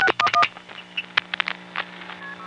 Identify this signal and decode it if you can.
DTMF, heard as telephone touch tones, tones of about 70 ms. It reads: 611